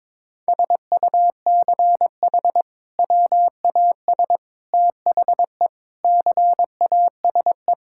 Morse code, heard as a keyed tone, about 22 wpm, 700 Hz. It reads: SUCH WAS THE CASE
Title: Sentences from Top 1600 Words 22wpm